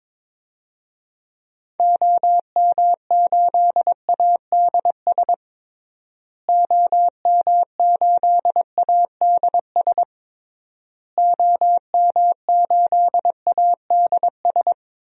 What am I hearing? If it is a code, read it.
Morse code, heard as a keyed tone, about 22 wpm, 700 Hz. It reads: OM8ADS OM8ADS OM8ADS
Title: DXCC-100 Call Signs - 6 Characters - Repeated 3x - 22wpm